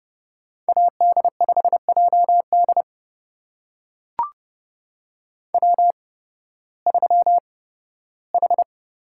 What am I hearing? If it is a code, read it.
Morse code, heard as a keyed tone, about 31 wpm, spaced as for 16 wpm, 700 Hz. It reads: AD5JD W3H
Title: USA Call Signs - 5 Characters - 8x Character Spacing - 30wpm